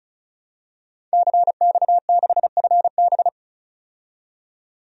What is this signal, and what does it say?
Morse code, heard as a keyed tone, about 34 wpm, 700 Hz. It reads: CX6FB
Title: DXCC-100 Call Signs - 5 Characters - 35wpm